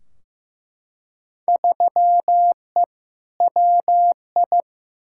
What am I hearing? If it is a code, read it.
Morse code, heard as a keyed tone, about 15 wpm, 700 Hz. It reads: 3E WI
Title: Field Day Exchanges Encore - 15wpm